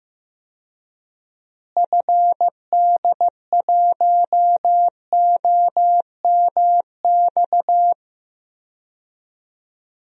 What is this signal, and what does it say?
Morse code, heard as a keyed tone, about 15 wpm, 700 Hz. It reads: FD1OMX
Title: DXCC-100 Call Signs - 6 Characters - Repeated 2x - 15wpm